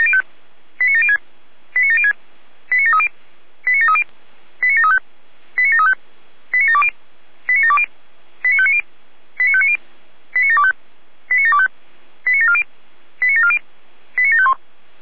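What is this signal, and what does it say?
Five-tone selective call (ZVEI), heard as quick runs of five tones, tones of about 70 ms. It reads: 89706 89706 89730 89730 89735 89735 89720 89720 89590 89590 89725 89725 89740 89740 89751
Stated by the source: ZVEI five-tone selective call received on SDR.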